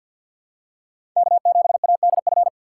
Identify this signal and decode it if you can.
Morse code, heard as a keyed tone, about 51 wpm, 700 Hz. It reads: K7ADF
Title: USA Call Signs - 5 Characters - 6x Character Spacing - 50wpm